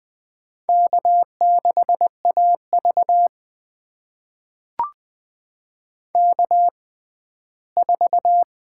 Morse code, heard as a keyed tone, about 20 wpm, spaced as for 14 wpm, 700 Hz. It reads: K6AV K4
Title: USA Call Signs - 6x Character Spacing - 20wpm